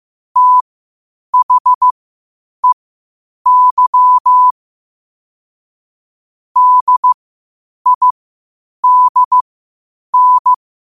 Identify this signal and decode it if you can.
Morse code, heard as a keyed tone, about 15 wpm, spaced as for 8 wpm, 1000 Hz. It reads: THEY DIDN